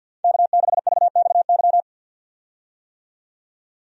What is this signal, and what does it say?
Morse code, heard as a keyed tone, about 52 wpm, 700 Hz. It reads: K6VXY
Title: USA Call Signs - 5 Characters - 50wpm